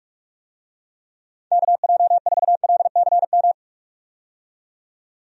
Morse code, heard as a keyed tone, about 46 wpm, 700 Hz. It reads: KJ4LCM
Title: USA Call Signs - 6 Characters - 45wpm